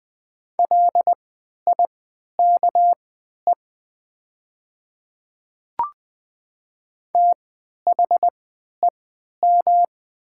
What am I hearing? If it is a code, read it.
Morse code, heard as a keyed tone, about 20 wpm, spaced as for 11 wpm, 700 Hz. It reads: LIKE THEM
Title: 3x Character Spacing - Top 100 Words - 20wpm